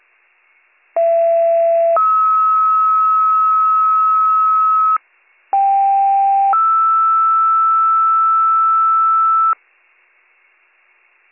Two-tone page, heard as a one-second tone, then a three-second tone, about 670 then 1290 Hz, about 790 then 1320 Hz. It is heard from 1.0 to 9.5 s.